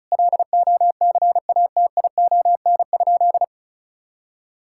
Morse code, heard as a keyed tone, about 36 wpm, 690 Hz. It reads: LOCATION?
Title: Sentences from Top 1000 Words - Review Part 6 - 35wpm